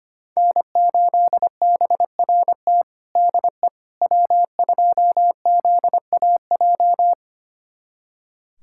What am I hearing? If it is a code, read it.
Morse code, heard as a keyed tone, about 25 wpm, 700 Hz. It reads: N8BRT DE W2ZAJ
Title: USA Callsign DE Callsign - 5 Characters - 25wpm